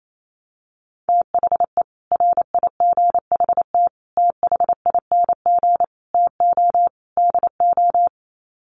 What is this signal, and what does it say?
Morse code, heard as a keyed tone, about 28 wpm, 700 Hz. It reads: THE RIGHT THING TO DO
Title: Sets of 5 Words 28wpm